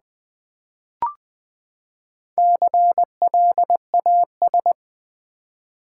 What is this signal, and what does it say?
Morse code, heard as a keyed tone, about 20 wpm, 700 Hz. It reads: CLAS
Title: Build A Word - 4 Letters At A Time - 8-Letter Words - 20wpm